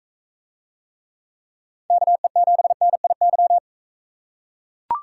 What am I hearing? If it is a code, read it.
Morse code, heard as a keyed tone, about 40 wpm, 700 Hz. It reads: KE7NIY